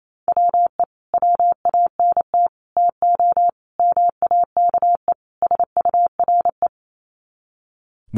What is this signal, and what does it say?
Morse code, heard as a keyed tone, about 28 wpm, 700 Hz. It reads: WE WANT TO MAKE SURE
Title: Sets of 5 Words Encore2 28wpm